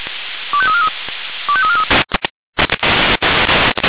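Five-tone selective call (ZVEI), heard as quick runs of five tones, tones of about 70 ms. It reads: 25343 25343
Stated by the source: ZVEI five-tone selective call received on SDR.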